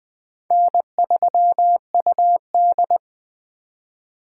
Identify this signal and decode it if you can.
Morse code, heard as a keyed tone, about 20 wpm, 700 Hz. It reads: N3UD
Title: 2023 POTA Activator Call Signs - 20wpm